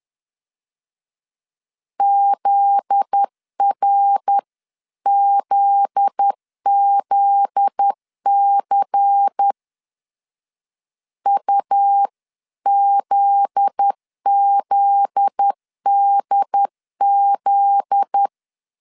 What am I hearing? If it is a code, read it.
Morse code, heard as a keyed tone, about 10 wpm, 800 Hz. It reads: ZRZZC UZZDZ
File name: LEZIONI / 7 GF9Z / 7 GF9Z 10wpm 2611